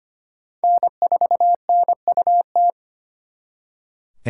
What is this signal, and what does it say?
Morse code, heard as a keyed tone, about 32 wpm, 700 Hz. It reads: N4NUT